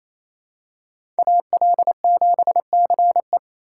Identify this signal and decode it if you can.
Morse code, heard as a keyed tone, about 28 wpm, 700 Hz. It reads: AL7CE
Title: USA Call Signs - 5 Characters - 4x Character Spacing - 28wpm